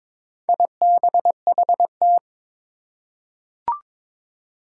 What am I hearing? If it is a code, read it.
Morse code, heard as a keyed tone, about 22 wpm, 700 Hz. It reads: IBHT
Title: Four Challenging Letter-Number Combinations 22wpm